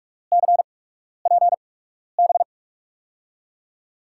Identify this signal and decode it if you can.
Morse code, heard as a keyed tone, about 44 wpm, spaced as for 12 wpm, 700 Hz. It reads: CPB
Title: USA Call Signs - 8x Character Spacing - 45wpm